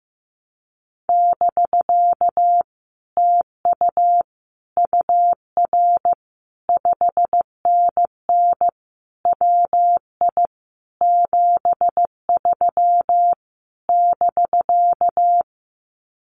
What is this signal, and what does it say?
Morse code, heard as a keyed tone, about 16 wpm, 700 Hz. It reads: <BK> TU UR 5NN WI 73 <BK>
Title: Common QSO Phrases Encore 15wpm